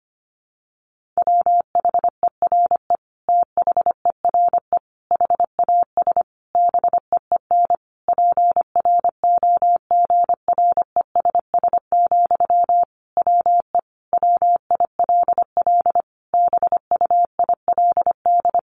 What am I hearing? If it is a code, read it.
Morse code, heard as a keyed tone, about 25 wpm, 700 Hz. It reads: WHERE THERE HAS BEEN PROGRESS, WE WILL BUILD
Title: Sentences from Top 1400 Words 25wpm